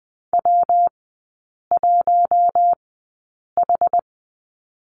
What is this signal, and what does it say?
Morse code, heard as a keyed tone, about 20 wpm, spaced as for 8 wpm, 690 Hz. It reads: W1H
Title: 159 Lesson 26e - Letter g - Three Characters 20wpm